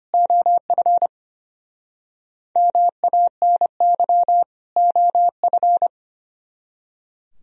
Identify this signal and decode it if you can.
Morse code, heard as a keyed tone, about 32 wpm, 700 Hz. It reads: OF MANY OF